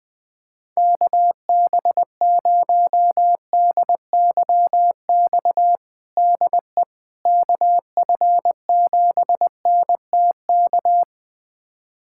Morse code, heard as a keyed tone, about 21 wpm, 700 Hz. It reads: KB0DYX DE KF7NTK
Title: USA Callsign DE Callsign - 6 Characters - 20wpm